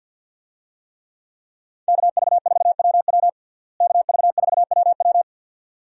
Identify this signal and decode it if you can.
Morse code, heard as a keyed tone, about 51 wpm, 700 Hz. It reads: KV4WW KV4WW
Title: USA Call Signs - 5 Characters - Repeated 2x - 50wpm